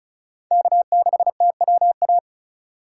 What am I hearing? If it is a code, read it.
Morse code, heard as a keyed tone, about 34 wpm, 700 Hz. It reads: K6TWA
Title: USA Call Signs - 5 Characters - 3x Character Spacing - 35wpm